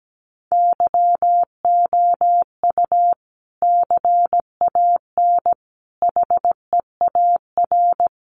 Morse code, heard as a keyed tone, about 17 wpm, 700 Hz. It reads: YOU CAN HEAR
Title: Sentences from Top 500 Words 17wpm